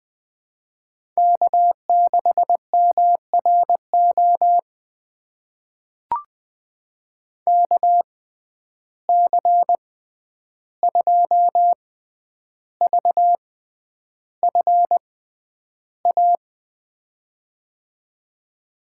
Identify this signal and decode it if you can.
Morse code, heard as a keyed tone, about 21 wpm, spaced as for 10 wpm, 700 Hz. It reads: K6MRO KC2VFA
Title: USA Call Signs - 6x Character Spacing - 20wpm